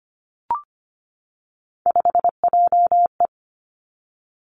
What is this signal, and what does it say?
Morse code, heard as a keyed tone, about 25 wpm, 700 Hz. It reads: 5JE
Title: Three Challenging Letter-Number Combinations 25wpm